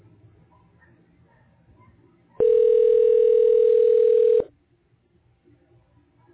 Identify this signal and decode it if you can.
Ringback tone (North America), 440 + 480 Hz.